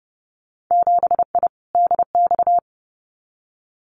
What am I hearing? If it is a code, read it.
Morse code, heard as a keyed tone, about 30 wpm, 700 Hz. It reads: 7I DX
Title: Winter Field Day Exchanges Encore 30wpm